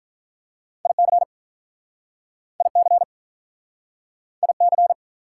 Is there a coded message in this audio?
Yes: Morse code from 0.9 to 4.9 s, heard as a keyed tone, about 48 wpm, 700 Hz. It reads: IC IC IC